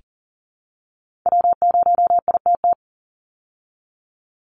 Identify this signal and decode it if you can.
Morse code, heard as a keyed tone, about 40 wpm, 700 Hz. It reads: W0ITT
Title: Call Signs Repeated 2x - 40wpm